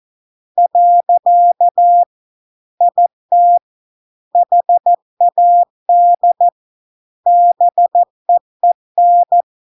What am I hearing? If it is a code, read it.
Morse code, heard as a keyed tone, about 14 wpm, 700 Hz. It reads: . IT HAD BEEN